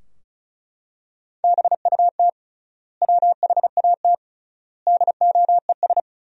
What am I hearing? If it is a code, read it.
Morse code, heard as a keyed tone, about 34 wpm, 700 Hz. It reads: BUT WHAT DOES
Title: 3x Word-Spacing - Sentences from Top 200 Words - 35wpm